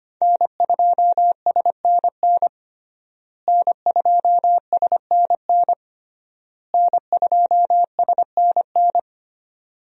Morse code, heard as a keyed tone, about 25 wpm, 700 Hz. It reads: N2SNN N2SNN N2SNN
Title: USA Call Signs - 5 Characters - Repeated 3x - 25wpm